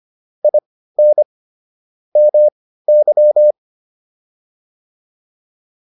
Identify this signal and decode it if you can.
Morse code, heard as a keyed tone, about 25 wpm, spaced as for 15 wpm, 600 Hz. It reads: IN MY